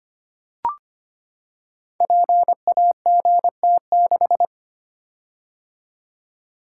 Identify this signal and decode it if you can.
Morse code, heard as a keyed tone, about 25 wpm, 690 Hz. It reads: PAGT6
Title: Code Groups - 5 Letter:Numbers - 25wpm